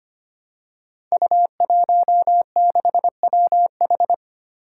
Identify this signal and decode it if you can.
Morse code, heard as a keyed tone, about 25 wpm, 700 Hz. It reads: U16WH
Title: Five Challenging Letter-Number Combinations 25wpm